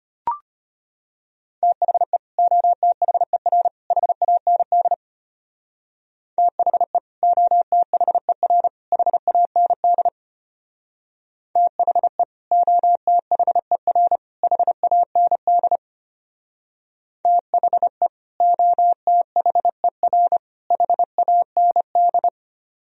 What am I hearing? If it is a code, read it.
Morse code, heard as a keyed tone, about 32 wpm, 700 Hz. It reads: THE OTHER HAND THE OTHER HAND THE OTHER HAND THE OTHER HAND